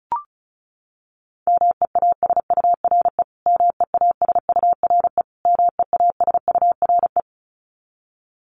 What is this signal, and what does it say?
Morse code, heard as a keyed tone, about 34 wpm, 700 Hz. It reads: MEASURE MEASURE MEASURE
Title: Top 500 Words Repeated 35wpm